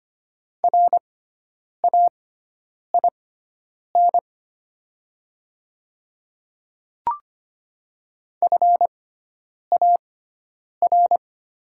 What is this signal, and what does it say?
Morse code, heard as a keyed tone, about 25 wpm, spaced as for 9 wpm, 690 Hz. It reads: RAIN FAR
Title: 6x Character Spacing - Top 500 Words - 25wpm